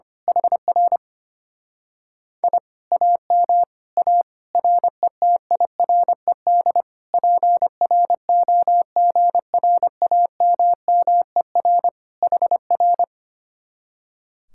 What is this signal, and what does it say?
Morse code, heard as a keyed tone, about 24 wpm, 700 Hz. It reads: HR I AM A RETIRED PROGRAMMER HR
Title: Speed-Racing Common QSO Phrases - 25wpm